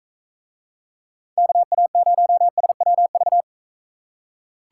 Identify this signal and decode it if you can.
Morse code, heard as a keyed tone, about 42 wpm, 700 Hz. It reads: KA0SWV